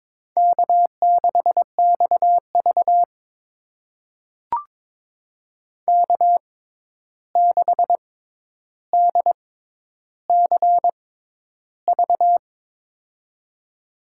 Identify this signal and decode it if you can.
Morse code, heard as a keyed tone, about 22 wpm, spaced as for 11 wpm, 700 Hz. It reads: K6XV K6DCV
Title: USA Call Signs - 6x Character Spacing - 22wpm